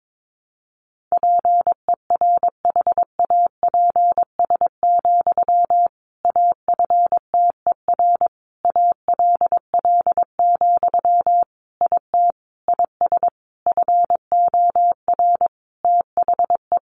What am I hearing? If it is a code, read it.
Morse code, heard as a keyed tone, about 22 wpm, 700 Hz. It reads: PERHAPS, AFTER ALL, IT IS FOR THE
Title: Sentences from Top 500 Words 22wpm